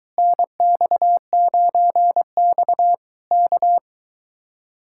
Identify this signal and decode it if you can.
Morse code, heard as a keyed tone, about 23 wpm, 700 Hz. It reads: NX9X K